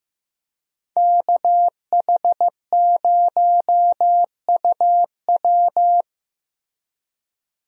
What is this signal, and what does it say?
Morse code, heard as a keyed tone, about 15 wpm, 700 Hz. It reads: KH0UW
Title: USA Call Signs - 5 Characters - Repeated 3x - 15wpm